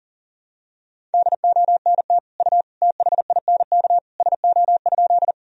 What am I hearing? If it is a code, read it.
Morse code, heard as a keyed tone, about 40 wpm, 700 Hz. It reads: DONT U THINK SO?
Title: Sentences from Top 300 Words and Abbreviations 40wpm